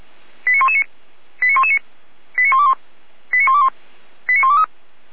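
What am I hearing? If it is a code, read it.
Five-tone selective call (ZVEI), heard as quick runs of five tones, tones of about 70 ms. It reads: 89119 89119 89121 89121 89123
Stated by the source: ZVEI five-tone call received by SDR, recorded by fire brigade Feuerwehr Orlamünde.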